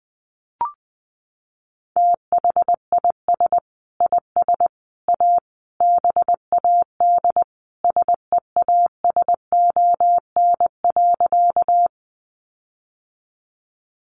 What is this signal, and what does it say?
Morse code, heard as a keyed tone, about 20 wpm, 700 Hz. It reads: THIS IS A BAD SEASON.
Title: Sentences from Top 1400 Words 20wpm